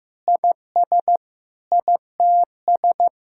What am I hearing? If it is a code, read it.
Morse code, heard as a keyed tone, about 15 wpm, 700 Hz. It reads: IS ITS
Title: Sentences from Top 1000 Words - Review Part 6 - 15wpm